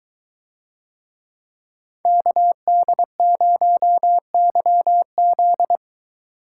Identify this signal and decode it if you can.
Morse code, heard as a keyed tone, about 23 wpm, 700 Hz. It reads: KD0YZ